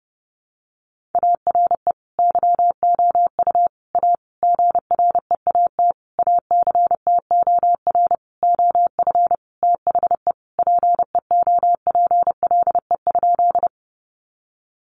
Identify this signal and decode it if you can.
Morse code, heard as a keyed tone, about 30 wpm, 700 Hz. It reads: ARE YOU A GREAT ACTOR OF THE PEOPLE?
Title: Sentences from Top 1500 Words 30wpm